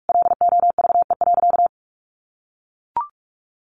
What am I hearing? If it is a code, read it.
Morse code, heard as a keyed tone, about 44 wpm, 700 Hz. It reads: LOVE.
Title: Sentences from Top 400 Words 45wpm